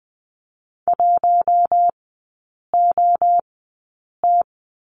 Morse code, heard as a keyed tone, about 20 wpm, spaced as for 8 wpm, 700 Hz. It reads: 1OT
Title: 067 Lesson 12e - Letter d - Three Characters 20wpm